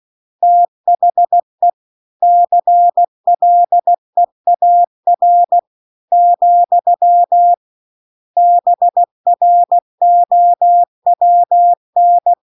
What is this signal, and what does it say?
Morse code, heard as a keyed tone, about 16 wpm, 700 Hz. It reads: THE CLEAR , BROWN
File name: CW-The-Big-Twoheaded-River-1-16WPM